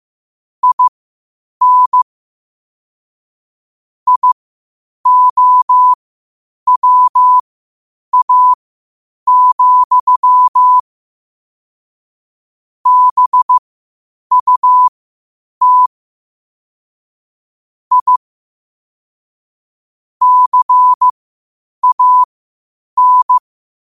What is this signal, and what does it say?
Morse code, heard as a keyed tone, about 15 wpm, spaced as for 8 wpm, 1000 Hz. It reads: IN IOWA, BUT I CAN